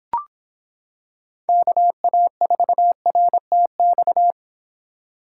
Morse code, heard as a keyed tone, about 26 wpm, 700 Hz. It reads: KA4RTX